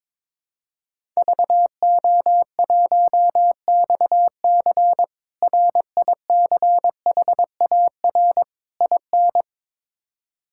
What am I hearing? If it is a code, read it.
Morse code, heard as a keyed tone, about 23 wpm, 700 Hz. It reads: VO1XC RICHAR IN
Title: K1USN SST QSO Practice - 22wpm